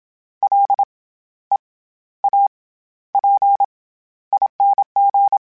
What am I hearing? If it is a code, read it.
Morse code, heard as a keyed tone, about 27 wpm, 800 Hz. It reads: L E A P ING
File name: Special INGs-mp3s / Special INGs-mp3s / SlowToFaster ing @25